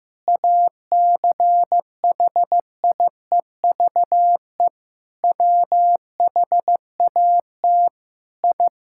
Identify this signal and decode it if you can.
Morse code, heard as a keyed tone, about 15 wpm, 700 Hz. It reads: ACHIEVE WHAT I
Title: Sentences from Top 1000 Words - Review Part 6 - 15wpm